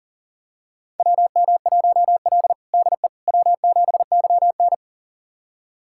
Morse code, heard as a keyed tone, about 40 wpm, 700 Hz. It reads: WM1L DE W7YN
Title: USA Callsign DE Callsign - 4 Characters - 40wpm